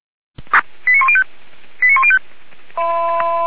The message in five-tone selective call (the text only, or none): 89105 89105